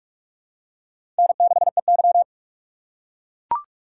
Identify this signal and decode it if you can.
Morse code, heard as a keyed tone, about 42 wpm, 700 Hz. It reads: N6EY
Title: USA Call Signs - 6x Character Spacing - 45wpm